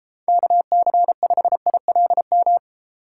Morse code, heard as a keyed tone, about 32 wpm, 700 Hz. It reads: KC5ILM